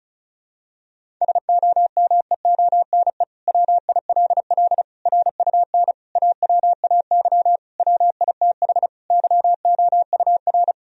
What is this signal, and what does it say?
Morse code, heard as a keyed tone, about 34 wpm, 700 Hz. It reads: SOMEONE WILL RUN AWAY WITH YOUR
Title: Sentences from Top 1000 Words - Review Part 1 - 35wpm